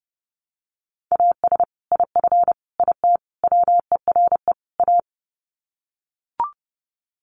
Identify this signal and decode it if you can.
Morse code, heard as a keyed tone, about 30 wpm, 700 Hz. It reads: AS IF IT WERE A
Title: Sets of 5 Words 30wpm